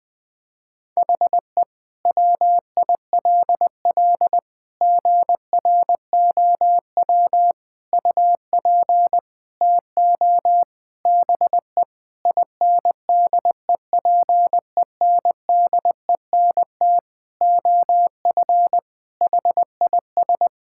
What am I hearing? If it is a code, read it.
Morse code, heard as a keyed tone, about 21 wpm, 700 Hz. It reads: HE WILL GROW UP TO BE INDEPENDENT OF HIS
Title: Sentences from Top 1000 Words - Review Part 6 - 20wpm